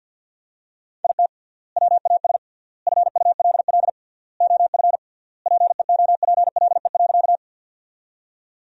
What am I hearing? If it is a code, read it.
Morse code, heard as a keyed tone, about 50 wpm, 700 Hz. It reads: IT WAS FULL OF PEOPLE.
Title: 3x Word-Spacing - Sentences from Top 500 Words - 50wpm